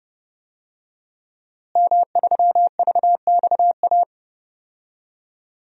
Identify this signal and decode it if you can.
Morse code, heard as a keyed tone, about 32 wpm, 700 Hz. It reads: M3VXA
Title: DXCC-100 Call Signs - 5 Characters - Repeated 2x - 30wpm